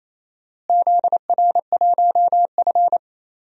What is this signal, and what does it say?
Morse code, heard as a keyed tone, about 27 wpm, 700 Hz. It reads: ZR1F
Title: DXCC-100 Call Signs - 4 Characters - Repeated 2x - 28wpm